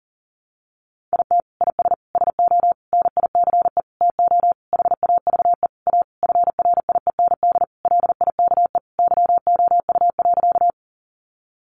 Morse code, heard as a keyed tone, about 40 wpm, 700 Hz. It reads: IT IS SO NICE TO HAVE A FRIEND LIKE YOU.
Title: Sentences from Top 1100 Words 40wpm